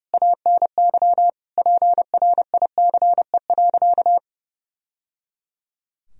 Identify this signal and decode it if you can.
Morse code, heard as a keyed tone, about 30 wpm, 700 Hz. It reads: ANY PRICE.
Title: Sentences from Top 1000 Words - Review Part 4 - 30wpm